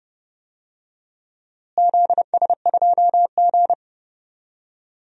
Morse code, heard as a keyed tone, about 32 wpm, 700 Hz. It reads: ZS2G
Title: DXCC-100 Call Signs - 4 Characters - Repeated 3x - 30wpm